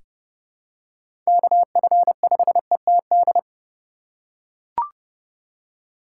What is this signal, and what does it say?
Morse code, heard as a keyed tone, about 30 wpm, 700 Hz. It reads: KF5ETD
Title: K1USN SST QSO Practice - 30wpm